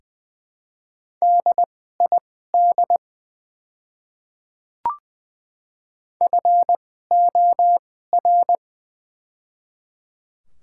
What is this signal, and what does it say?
Morse code, heard as a keyed tone, about 21 wpm, spaced as for 15 wpm, 700 Hz. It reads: DID FOR
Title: 2x Character Spacing - Top 100 Words - 20wpm